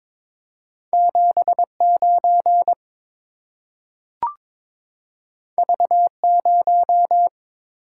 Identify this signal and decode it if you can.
Morse code, heard as a keyed tone, about 22 wpm, 690 Hz. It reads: 79 V0
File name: practice / 22wpm / Two Letter-Number Combinations - 22wpm